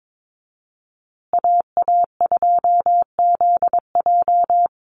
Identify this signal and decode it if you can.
Morse code, heard as a keyed tone, about 22 wpm, 700 Hz. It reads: AA2ZJ
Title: USA Call Signs Repeated 2x 22wpm